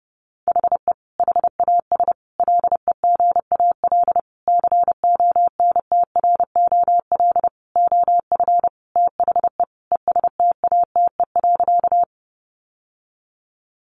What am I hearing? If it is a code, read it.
Morse code, heard as a keyed tone, about 29 wpm, 700 Hz. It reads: HE HAS LEGAL CONTROL OF THE ESTATE.
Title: Sentences from Top 1200 Words 30wpm